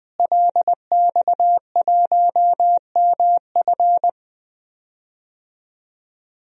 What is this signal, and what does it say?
Morse code, heard as a keyed tone, about 21 wpm, 700 Hz. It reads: LX1MF
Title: DXCC-100 Call Signs - 5 Characters - Repeated 2x - 20wpm